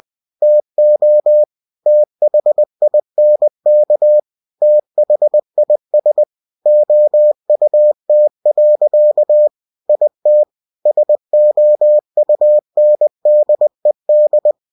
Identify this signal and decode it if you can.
Morse code, heard as a keyed tone, about 20 wpm, 600 Hz. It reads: TO THINK THIS OUT. IT SOUNDED
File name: cw-books / winnie-the-pooh / winnie-20wpm-page-0045